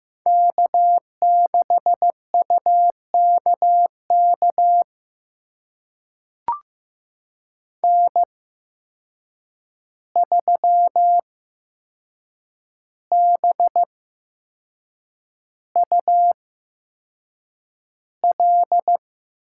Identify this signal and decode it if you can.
Morse code, heard as a keyed tone, about 15 wpm, spaced as for 6 wpm, 700 Hz. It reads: K6UKK N3BUL
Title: USA Call Signs - 5 Characters - 8x Character Spacing - 15wpm